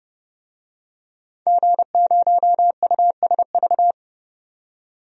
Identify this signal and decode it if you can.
Morse code, heard as a keyed tone, about 30 wpm, 700 Hz. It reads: G0USV
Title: DXCC-100 Call Signs - 5 Characters - 30wpm